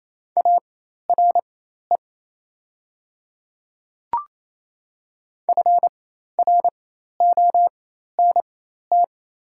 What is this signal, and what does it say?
Morse code, heard as a keyed tone, about 27 wpm, spaced as for 13 wpm, 700 Hz. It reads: ARE FRONT
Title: 4x Character Spacing - Top 500 Words - 28wpm